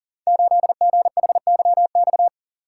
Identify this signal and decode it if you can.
Morse code, heard as a keyed tone, about 42 wpm, 700 Hz. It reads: 8GHYX
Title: Code Groups - 5 Letter:Numbers - 40wpm